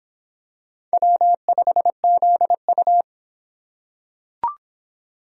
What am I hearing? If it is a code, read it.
Morse code, heard as a keyed tone, about 25 wpm, 700 Hz. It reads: W5ZU